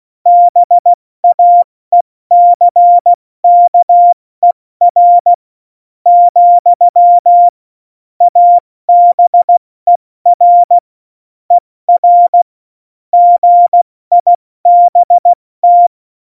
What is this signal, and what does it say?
Morse code, heard as a keyed tone, about 16 wpm, 700 Hz. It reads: BAECKER , ABER ER GIBT
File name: Morsekurs / Witze / Witz 11. September 2018, 70 F80